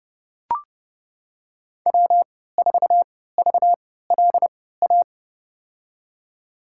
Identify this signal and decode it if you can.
Morse code, heard as a keyed tone, about 30 wpm, spaced as for 17 wpm, 700 Hz. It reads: W4VLA
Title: USA Call Signs - 5 Characters - 3x Character Spacing - 30wpm